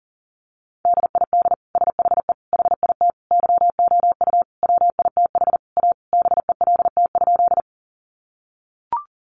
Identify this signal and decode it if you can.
Morse code, heard as a keyed tone, about 40 wpm, 700 Hz. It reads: DID SHE HIT YOU WITH A BELT?
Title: Sentences from Top 1400 Words 40wpm